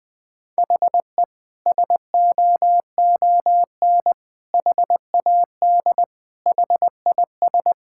Morse code, heard as a keyed tone, about 20 wpm, 700 Hz. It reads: HE SOON HAD HIS
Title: Sentences from Top 1600 Words 20wpm